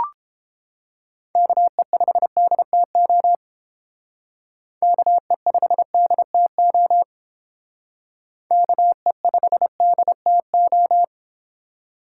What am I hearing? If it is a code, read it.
Morse code, heard as a keyed tone, about 28 wpm, 700 Hz. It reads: KE5DTO KE5DTO KE5DTO